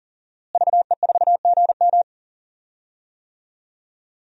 Morse code, heard as a keyed tone, about 40 wpm, 700 Hz. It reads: VE4GM
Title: DXCC-100 Call Signs - 5 Characters - Repeated 3x - 40wpm